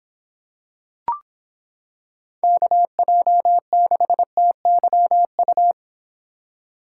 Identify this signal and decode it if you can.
Morse code, heard as a keyed tone, about 26 wpm, 700 Hz. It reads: KJ6TYU